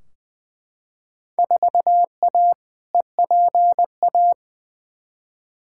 Morse code, heard as a keyed tone, about 20 wpm, 700 Hz. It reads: 4A EPA
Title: Field Day Exchanges - 20wpm